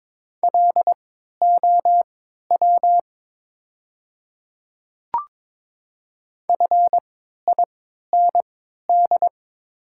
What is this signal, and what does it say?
Morse code, heard as a keyed tone, about 22 wpm, spaced as for 12 wpm, 700 Hz. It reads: LOW FIND
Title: 3x Character Spacing - Top 200 Words - 22wpm